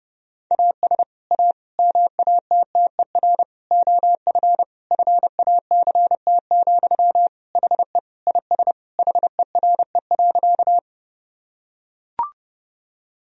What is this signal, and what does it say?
Morse code, heard as a keyed tone, about 32 wpm, 700 Hz. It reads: AS A MATTER OF FACT, HE IS HERE.
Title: Sentences from Top 1000 Words - Review Part 3 - 30wpm